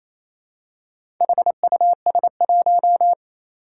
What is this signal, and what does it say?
Morse code, heard as a keyed tone, about 28 wpm, 700 Hz. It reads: HUS1
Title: Four Challenging Letter-Number Combinations 28wpm